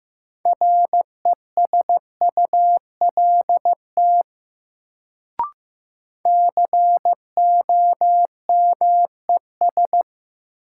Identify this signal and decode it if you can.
Morse code, heard as a keyed tone, about 15 wpm, 700 Hz. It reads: RESULT COMES
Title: Sets of 3 Words - New Word Drill - 15wpm